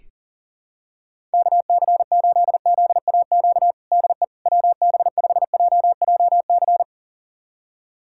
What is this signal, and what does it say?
Morse code, heard as a keyed tone, about 40 wpm, 700 Hz. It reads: KC8ZAQ DE WB5JJC
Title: USA Callsign DE Callsign 40wpm